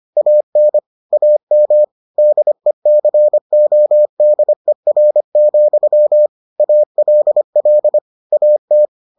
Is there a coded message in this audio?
Yes: Morse code from 0.2 to 8.9 s, heard as a keyed tone, about 25 wpm, 600 Hz. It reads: AN AM DECODER, ALL AT